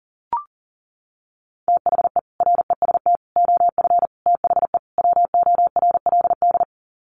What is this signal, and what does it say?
Morse code, heard as a keyed tone, about 40 wpm, 700 Hz. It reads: THE REST OF THE WORLD
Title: Sets of 5 Words 40wpm